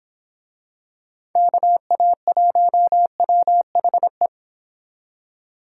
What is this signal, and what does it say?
Morse code, heard as a keyed tone, about 27 wpm, 700 Hz. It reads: KA1WHE
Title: Speed Racing - USA 2x3 Call Signs 22wpm